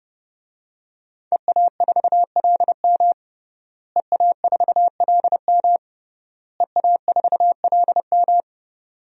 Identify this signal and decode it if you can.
Morse code, heard as a keyed tone, about 30 wpm, 700 Hz. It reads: EA4LM EA4LM EA4LM
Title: DXCC-100 Call Signs - 5 Characters - Repeated 3x - 30wpm